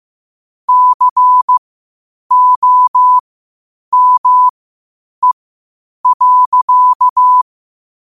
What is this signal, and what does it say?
Morse code, heard as a keyed tone, about 15 wpm, spaced as for 8 wpm, 1000 Hz. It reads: COME.
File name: qotdfiles / 8wpm